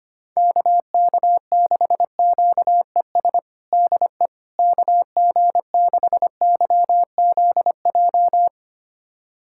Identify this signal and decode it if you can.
Morse code, heard as a keyed tone, about 25 wpm, 700 Hz. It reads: KK6QES DE KG6YZJ
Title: USA Callsign DE Callsign - 6 Characters - 25wpm